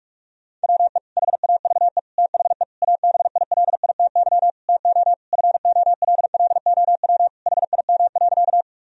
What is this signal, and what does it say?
Morse code, heard as a keyed tone, about 46 wpm, 700 Hz. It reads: WE HAVE THE ABILITY TO FOLLOW HIM.
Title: Sentences from Top 1000 Words - Review Part 3 - 45wpm